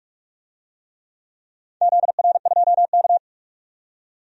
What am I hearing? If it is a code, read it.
Morse code, heard as a keyed tone, about 46 wpm, 690 Hz. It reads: ZR2K
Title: DXCC-100 Call Signs - 4 Characters - 45wpm